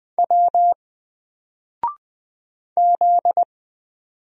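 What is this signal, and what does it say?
Morse code, heard as a keyed tone, about 20 wpm, 700 Hz. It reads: W Z
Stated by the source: Single letter and number Morse drill at 20 wpm, from N4LCD.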